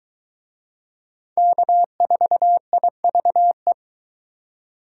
Morse code, heard as a keyed tone, about 21 wpm, 700 Hz. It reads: K4IVE